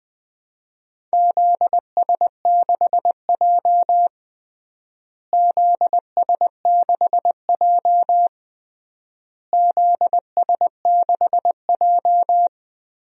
Morse code, heard as a keyed tone, about 20 wpm, 700 Hz. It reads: ZS6J ZS6J ZS6J
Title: DXCC-100 Call Signs - 4 Characters - Repeated 3x - 20wpm